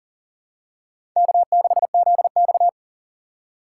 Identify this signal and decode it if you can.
Morse code, heard as a keyed tone, about 39 wpm, 700 Hz. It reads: K6ZX
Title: K1USN SST QSO Practice - 40wpm